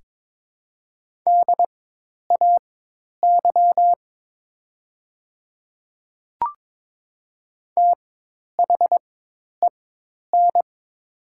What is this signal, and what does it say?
Morse code, heard as a keyed tone, about 22 wpm, spaced as for 10 wpm, 700 Hz. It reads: DAY THEN
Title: 4x Character Spacing - Top 100 Words - 22wpm